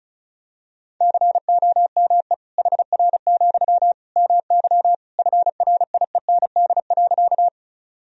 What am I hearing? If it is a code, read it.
Morse code, heard as a keyed tone, about 36 wpm, 700 Hz. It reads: COME HR, MY FRIEND.
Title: Sentences from Top 300 Words and Abbreviations 35wpm